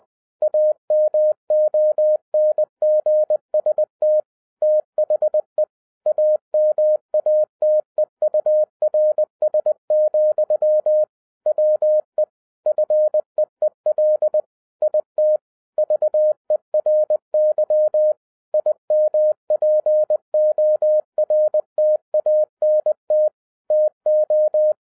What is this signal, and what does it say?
Morse code, heard as a keyed tone, about 20 wpm, 610 Hz. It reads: AMONGST THE AMATEURS, WE FEEL IT VERY IMPORTANT TO